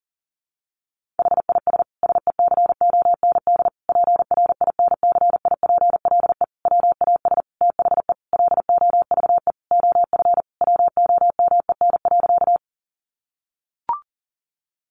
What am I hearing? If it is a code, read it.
Morse code, heard as a keyed tone, about 40 wpm, 700 Hz. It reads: HIS SECOND PRINCIPLE WAS THE LOVE OF WOMEN.
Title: Sentences from Top 1400 Words 40wpm